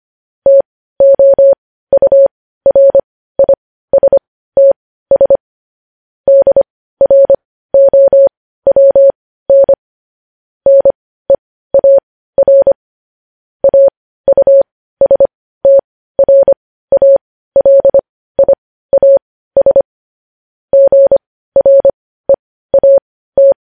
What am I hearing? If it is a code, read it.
Morse code, heard as a keyed tone, about 25 wpm, spaced as for 15 wpm, 550 Hz. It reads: TOURISTS DROWN NEAR AUSTRALIAS GREAT
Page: News Headlines in Morse Code at 15 WPM